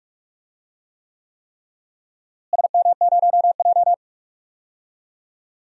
Morse code, heard as a keyed tone, about 46 wpm, 700 Hz. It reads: SM0J
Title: DXCC-100 Call Signs - 4 Characters - Repeated 3x - 45wpm